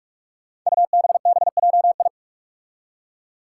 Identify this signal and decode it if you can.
Morse code, heard as a keyed tone, about 46 wpm, 700 Hz. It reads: UBBJI
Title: Five Challenging Letter-Number Combinations 45wpm